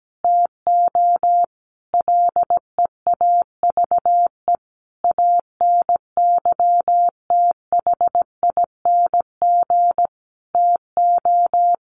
Morse code, heard as a keyed tone, about 17 wpm, 700 Hz. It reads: TO LEAVE ANYTHING TO
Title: Sentences from Top 1100 Words 17wpm